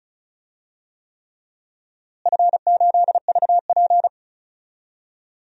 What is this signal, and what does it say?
Morse code, heard as a keyed tone, about 36 wpm, 700 Hz. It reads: F8VP
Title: DXCC-100 Call Signs - 4 Characters - Repeated 3x - 35wpm